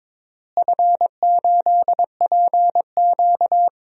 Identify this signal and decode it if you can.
Morse code, heard as a keyed tone, about 22 wpm, 700 Hz. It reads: F8PQ
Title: DXCC-100 Call Signs - 4 Characters - Repeated 3x - 22wpm